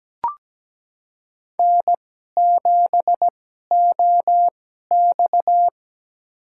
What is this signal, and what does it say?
Morse code, heard as a keyed tone, about 17 wpm, spaced as for 12 wpm, 700 Hz. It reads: N7OX
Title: USA Call Signs - 4 Characters - 2x Character Spacing - 17wpm